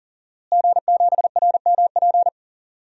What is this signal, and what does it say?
Morse code, heard as a keyed tone, about 40 wpm, 700 Hz. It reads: G7RMP
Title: DXCC-100 Call Signs - 5 Characters - Repeated 3x - 40wpm